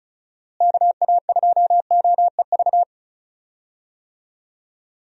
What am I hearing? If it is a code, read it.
Morse code, heard as a keyed tone, about 34 wpm, 700 Hz. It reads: KA2OEV
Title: USA Call Signs - 6 Characters - Repeated 2x - 35wpm